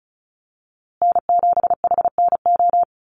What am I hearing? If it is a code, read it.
Morse code, heard as a keyed tone, about 34 wpm, 700 Hz. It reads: N7HNO
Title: USA Call Signs Repeated 2x 35wpm